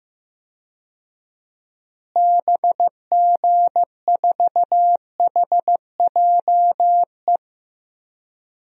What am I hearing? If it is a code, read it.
Morse code, heard as a keyed tone, about 16 wpm, 700 Hz. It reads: BG4HJE
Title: DXCC-100 Call Signs - 6 Characters - Repeated 3x - 15wpm